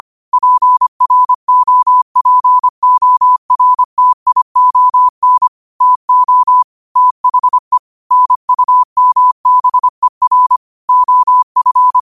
Morse code, heard as a keyed tone, about 25 wpm, 1000 Hz. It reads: PROPORTION TO THE NUMBER OF
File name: qotdfiles / 25wpm